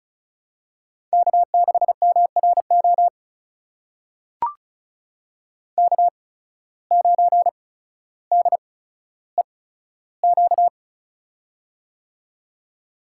Morse code, heard as a keyed tone, about 36 wpm, spaced as for 15 wpm, 700 Hz. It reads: K6MRO K9DEQ
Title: USA Call Signs - 8x Character Spacing - 35wpm